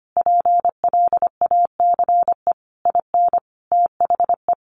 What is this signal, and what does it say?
Morse code, heard as a keyed tone, about 25 wpm, 700 Hz. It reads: PLACE IN THE
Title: Sentences from Top 1300 Words 25wpm